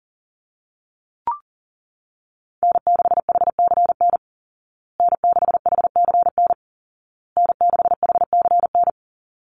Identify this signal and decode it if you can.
Morse code, heard as a keyed tone, about 40 wpm, 700 Hz. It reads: N6HCN N6HCN N6HCN
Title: Call Signs Repeated 2x - 40wpm